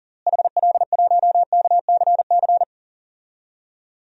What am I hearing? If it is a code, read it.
Morse code, heard as a keyed tone, about 38 wpm, 700 Hz. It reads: HL1KCC
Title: DXCC-100 Call Signs - 6 Characters - Repeated 3x - 40wpm